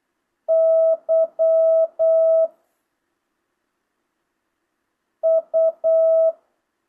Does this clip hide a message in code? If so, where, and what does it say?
Yes: Morse code from 0.5 to 6.3 s, heard as a keyed tone, about 8 wpm, 660 Hz. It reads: Y U